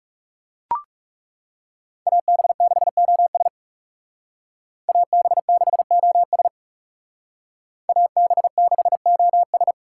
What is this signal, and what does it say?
Morse code, heard as a keyed tone, about 37 wpm, 700 Hz. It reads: AB6OS AB6OS AB6OS
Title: Speed Racing - USA Call Signs 30wpm